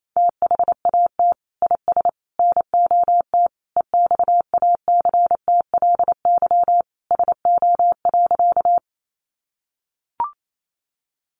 Morse code, heard as a keyed tone, about 28 wpm, 700 Hz. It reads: THAT IS NOT EXACTLY SO.
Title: Sentences from Top 900 Words 28wpm